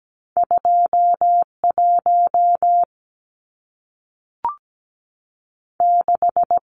Morse code, heard as a keyed tone, about 17 wpm, 700 Hz. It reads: 2 1 6
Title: Numbers - 2-Digit 17wpm